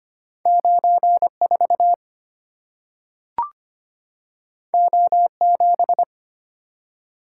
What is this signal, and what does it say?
Morse code, heard as a keyed tone, about 25 wpm, 700 Hz. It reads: 94 O7
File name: Two Letter-Number Combinations - 25wpm